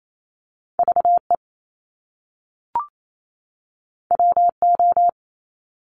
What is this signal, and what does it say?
Morse code, heard as a keyed tone, about 28 wpm, 700 Hz. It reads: V E W O
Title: Top Two Letter Combinations in Words 28wpm